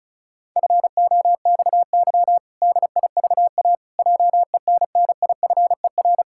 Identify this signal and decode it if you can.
Morse code, heard as a keyed tone, about 34 wpm, 700 Hz. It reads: FOXY DIVA JENNIFER
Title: Pangram Challenge v2 - 35wpm